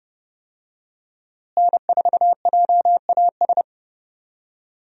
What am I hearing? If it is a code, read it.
Morse code, heard as a keyed tone, about 30 wpm, 700 Hz. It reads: N4JAS
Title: USA Call Signs - 5 Characters - Repeated 3x - 30wpm